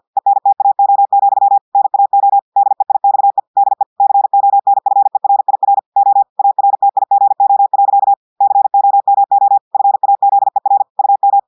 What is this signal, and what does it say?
Morse code, heard as a keyed tone, about 50 wpm, 800 Hz. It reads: ENTAO, NAO DEIXE DE CONFERIR O ARTIGO. COMO FAZER UM
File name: portugais texte en clair 2 vitesse 50 mots minutes